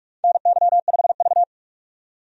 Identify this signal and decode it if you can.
Morse code, heard as a keyed tone, about 46 wpm, 700 Hz. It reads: NY5V
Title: USA Call Signs - 4 Characters - 45wpm